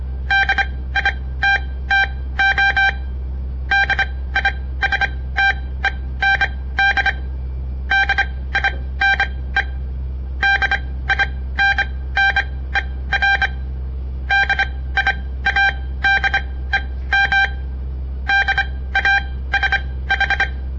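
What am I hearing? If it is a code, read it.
Morse code, heard as a keyed tone, about 25 wpm, spaced as for 16 wpm, 1600 Hz. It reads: DITTO DISTEND DINE DINNER DIADEM DASH
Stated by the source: On-air Morse code practice broadcast by a southern Montana amateur radio association; all code, no read back.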